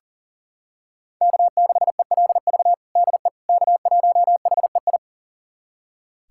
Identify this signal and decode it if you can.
Morse code, heard as a keyed tone, about 40 wpm, 700 Hz. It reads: K6ELV DE K1HEI
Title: USA Callsign DE Callsign - 5 Characters - 40wpm